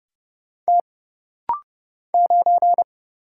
Morse code, heard as a keyed tone, about 28 wpm, 700 Hz. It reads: T 9
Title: Single Letter-Number - Rapid-Fire 30wpm